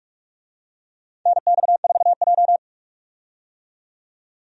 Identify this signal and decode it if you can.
Morse code, heard as a keyed tone, about 45 wpm, 700 Hz. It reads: NX4J
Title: USA Call Signs - 4 Characters - 45wpm